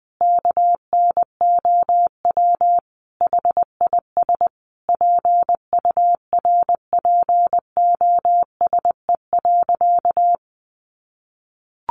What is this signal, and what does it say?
Morse code, heard as a keyed tone, about 20 wpm, 700 Hz. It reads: KNOW HIS PURPOSE.
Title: Sentences from Top 800 Words 20wpm